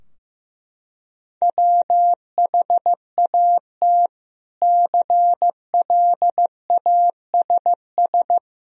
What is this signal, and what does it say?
Morse code, heard as a keyed tone, about 15 wpm, 700 Hz. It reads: WHAT CLASS
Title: Sentences from Top 1000 Words - Review Part 2 - 15wpm